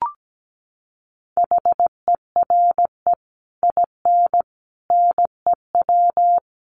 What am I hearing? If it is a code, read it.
Morse code, heard as a keyed tone, about 17 wpm, 700 Hz. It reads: HERE IN NEW
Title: Sets of 5 Words Encore2 17wpm